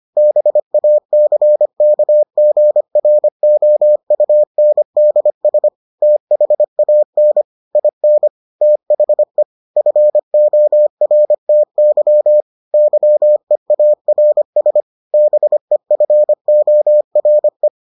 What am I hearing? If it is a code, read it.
Morse code, heard as a keyed tone, about 25 wpm, 600 Hz. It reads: BACKGROUNDS THAN IN THE FORTY YEARS BEFORE